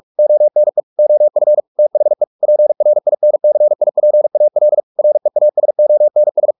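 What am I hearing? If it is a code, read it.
Morse code, heard as a keyed tone, about 44 wpm, 600 Hz. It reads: ONE OF THE PRINCIPAL REASONS
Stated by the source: Morse practice at 45 wpm, the novel Never by Ken Follett in English, from CW med Gnister.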